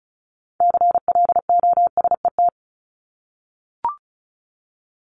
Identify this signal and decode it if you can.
Morse code, heard as a keyed tone, about 34 wpm, 700 Hz. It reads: CLOSET
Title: Top 1751-2000 Words 35wpm